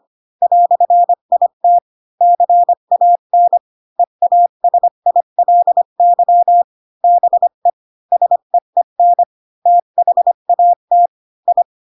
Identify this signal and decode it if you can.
Morse code, heard as a keyed tone, about 25 wpm, 700 Hz. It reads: "IT CAN EASILY BE SEEN THAT I
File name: Pinocchio / Pinocchio ch1 20 25 ik1hge